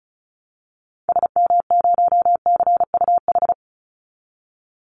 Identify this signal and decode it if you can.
Morse code, heard as a keyed tone, about 34 wpm, 700 Hz. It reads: SM0CUH
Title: Call Signs Repeated 2x - 35wpm